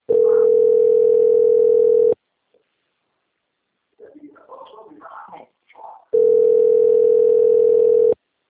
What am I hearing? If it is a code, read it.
Ringback tone (North America), 440 + 480 Hz, with speech.
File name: out- - - - - .3542